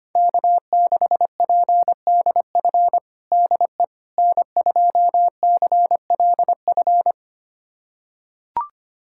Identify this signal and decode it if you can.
Morse code, heard as a keyed tone, about 25 wpm, 700 Hz. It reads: K6PDF DE N2CLF
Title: USA Callsign DE Callsign - 5 Characters - 25wpm